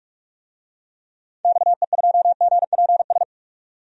Morse code, heard as a keyed tone, about 41 wpm, 700 Hz. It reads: XE2GPS
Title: DXCC-100 Call Signs - 6 Characters - 45wpm